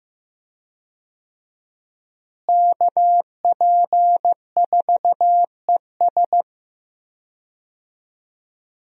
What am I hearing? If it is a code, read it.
Morse code, heard as a keyed tone, about 15 wpm, 700 Hz. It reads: KP4ES
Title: USA Call Signs - 5 Characters - Repeated 3x - 15wpm